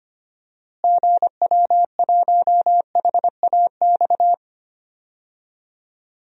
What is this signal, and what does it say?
Morse code, heard as a keyed tone, about 25 wpm, 700 Hz. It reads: GW1HAX
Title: DXCC-100 Call Signs - 6 Characters - Repeated 3x - 25wpm